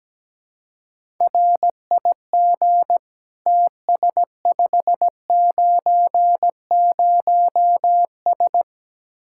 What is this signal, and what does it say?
Morse code, heard as a keyed tone, about 17 wpm, 700 Hz. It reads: RIG TS590S
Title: Speed-Racing Common QSO Phrases - 17wpm